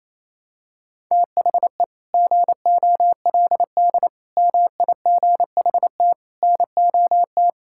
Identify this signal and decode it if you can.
Morse code, heard as a keyed tone, about 28 wpm, 700 Hz. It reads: THE GOLD MIGHT NOT
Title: Sentences from Top 1000 Words - Review Part 5 - 28wpm